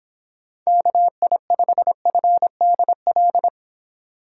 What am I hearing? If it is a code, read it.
Morse code, heard as a keyed tone, about 25 wpm, 700 Hz. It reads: KI5FDL